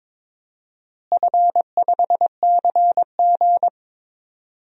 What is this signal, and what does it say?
Morse code, heard as a keyed tone, about 22 wpm, 700 Hz. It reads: F5CG
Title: DXCC-100 Call Signs - 4 Characters - Repeated 3x - 22wpm